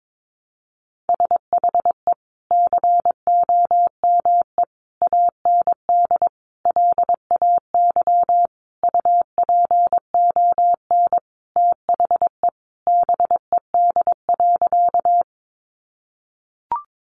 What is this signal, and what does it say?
Morse code, heard as a keyed tone, about 22 wpm, 700 Hz. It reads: SHE COME AND LAY UPON THE BED.
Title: Sentences from Top 1100 Words 22wpm